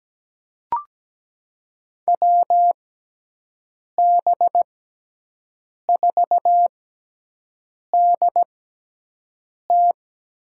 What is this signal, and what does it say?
Morse code, heard as a keyed tone, about 17 wpm, spaced as for 6 wpm, 700 Hz. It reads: WB4DT
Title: USA Call Signs - 5 Characters - 6x Character Spacing - 17wpm